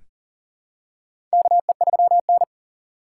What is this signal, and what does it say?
Morse code, heard as a keyed tone, about 40 wpm, 700 Hz. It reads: KE3N
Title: USA Call Signs - 4 Characters - 4x Character Spacing - 40wpm